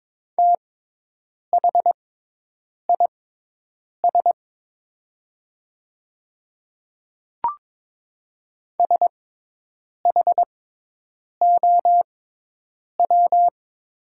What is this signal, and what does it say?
Morse code, heard as a keyed tone, about 22 wpm, spaced as for 8 wpm, 700 Hz. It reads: THIS SHOW
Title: 6x Character Spacing - Top 200 Words - 22wpm